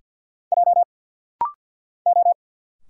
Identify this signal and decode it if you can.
Morse code, heard as a keyed tone, about 50 wpm, 700 Hz. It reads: J O
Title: Single Letter-Number Alt - Rapid-Fire 50wpm